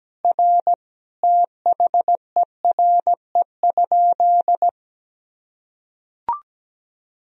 Morse code, heard as a keyed tone, about 17 wpm, 700 Hz. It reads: R THERE?
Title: Sentences from Top 100 Words and Abbreviations 17wpm